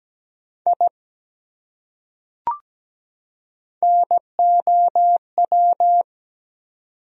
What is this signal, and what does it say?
Morse code, heard as a keyed tone, about 17 wpm, 700 Hz. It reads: I NOW
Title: Top 100 Words 17wpm